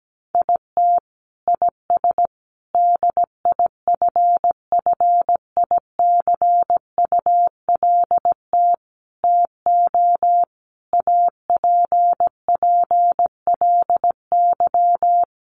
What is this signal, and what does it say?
Morse code, heard as a keyed tone, about 17 wpm, 700 Hz. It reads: IT IS DIFFICULT TO APPLY
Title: Sentences from Top 1200 Words 17wpm